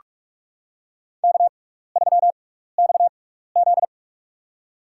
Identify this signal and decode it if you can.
Morse code, heard as a keyed tone, about 46 wpm, spaced as for 16 wpm, 700 Hz. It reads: K3XZ
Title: USA Call Signs - 4 Characters - 6x Character Spacing - 45wpm